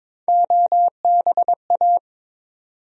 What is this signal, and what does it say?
Morse code, heard as a keyed tone, about 22 wpm, 700 Hz. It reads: OBA
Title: USA Call Sign Postfixes - 3 Letters 22wpm